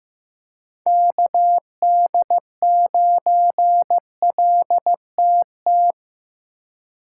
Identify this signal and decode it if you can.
Morse code, heard as a keyed tone, about 18 wpm, 700 Hz. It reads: KD9LTT
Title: Speed Racing - USA 2x3 Call Signs 15wpm